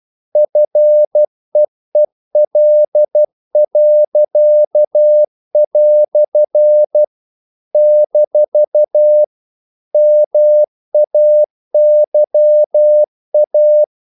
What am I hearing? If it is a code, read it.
Morse code, heard as a keyed tone, about 12 wpm, 600 Hz. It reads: FEEL." - MAYA
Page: English quotations in CW (telegraphy - morse)